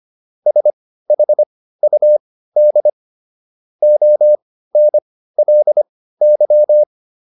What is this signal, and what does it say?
Morse code, heard as a keyed tone, about 25 wpm, spaced as for 15 wpm, 600 Hz. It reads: SHUD ONLY